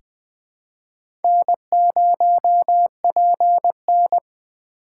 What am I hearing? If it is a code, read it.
Morse code, heard as a keyed tone, about 20 wpm, 700 Hz. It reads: N0PN
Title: USA Call Signs - 4 Characters - 6x Character Spacing - 20wpm